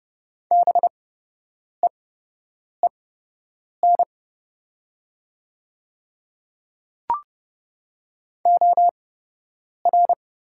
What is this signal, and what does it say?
Morse code, heard as a keyed tone, about 31 wpm, spaced as for 8 wpm, 700 Hz. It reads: BEEN OR
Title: 8x Character Spacing - Top 100 Words - 30wpm